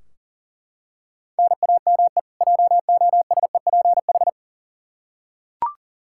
Morse code, heard as a keyed tone, about 37 wpm, 700 Hz. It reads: NAME JOSEPH
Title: K1USN SST QSO Practice 40wpm